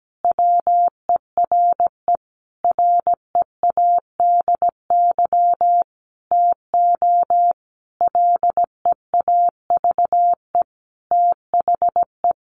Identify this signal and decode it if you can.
Morse code, heard as a keyed tone, about 17 wpm, 700 Hz. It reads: WERE READY TO LEAVE THE
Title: A Sound Of Thunder 17wpm